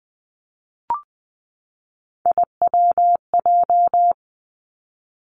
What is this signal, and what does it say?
Morse code, heard as a keyed tone, about 20 wpm, 700 Hz. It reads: IWJ
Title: Three Challenging Letter-Number Combinations 20wpm